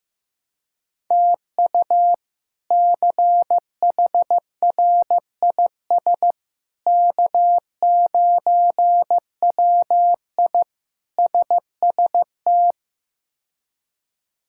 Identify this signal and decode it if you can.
Morse code, heard as a keyed tone, about 16 wpm, 700 Hz. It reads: TU CHRIS K9WI SST
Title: K1USN SST QSO Practice - 15wpm